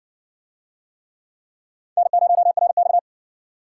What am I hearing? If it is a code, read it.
Morse code, heard as a keyed tone, about 60 wpm, 700 Hz. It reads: N0RX